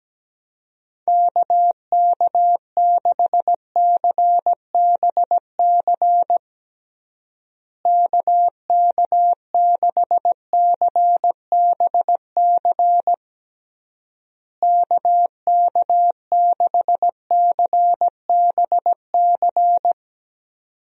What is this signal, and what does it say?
Morse code, heard as a keyed tone, about 17 wpm, 700 Hz. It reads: KK6CBC KK6CBC KK6CBC
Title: USA Call Signs - 6 Characters - Repeated 3x - 17wpm